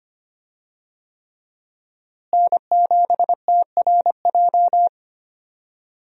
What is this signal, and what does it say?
Morse code, heard as a keyed tone, about 25 wpm, 700 Hz. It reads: N7TRJ
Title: USA Call Signs - 5 Characters - Repeated 3x - 25wpm